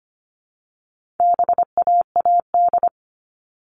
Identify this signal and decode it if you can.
Morse code, heard as a keyed tone, about 25 wpm, 700 Hz. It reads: BAAD
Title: All Cut Numbers - 4-Digit 25wpm